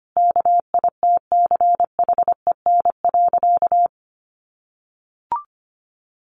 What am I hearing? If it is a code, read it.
Morse code, heard as a keyed tone, about 25 wpm, 700 Hz. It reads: KITCHEN.
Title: Sentences from Top 1400 Words 25wpm